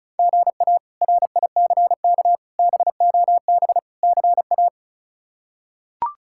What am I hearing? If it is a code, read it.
Morse code, heard as a keyed tone, about 36 wpm, 700 Hz. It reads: GA RICK BOB CA
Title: K1USN SST QSO Practice - 35wpm